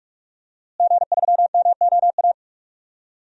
Morse code, heard as a keyed tone, about 46 wpm, 700 Hz. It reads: G3MOA
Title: DXCC-100 Call Signs - 5 Characters - Repeated 3x - 45wpm